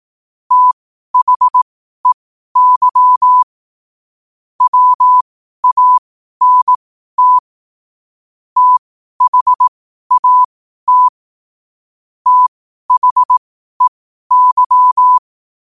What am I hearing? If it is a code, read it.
Morse code, heard as a keyed tone, about 18 wpm, spaced as for 12 wpm, 1000 Hz. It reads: THEY WANT THAT THEY
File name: qotdfiles / 12wpm